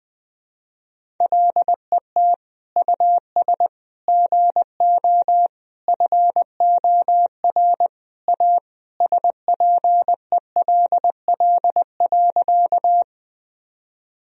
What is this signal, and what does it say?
Morse code, heard as a keyed tone, about 20 wpm, 700 Hz. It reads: LET US GO FOR A SPELL.
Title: Sentences from Top 1000 Words - Review Part 1 - 20wpm